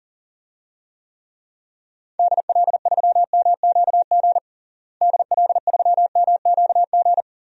Morse code, heard as a keyed tone, about 40 wpm, 700 Hz. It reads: DL3MQG DL3MQG
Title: DXCC-100 Call Signs - 6 Characters - Repeated 3x - 40wpm